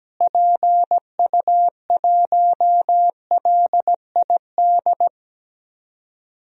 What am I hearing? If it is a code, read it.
Morse code, heard as a keyed tone, about 17 wpm, 700 Hz. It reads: PU1LID
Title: DXCC-100 Call Signs - 6 Characters - Repeated 3x - 17wpm